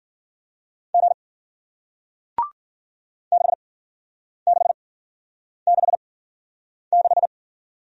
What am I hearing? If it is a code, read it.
Morse code, heard as a keyed tone, about 53 wpm, spaced as for 9 wpm, 700 Hz. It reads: G 6666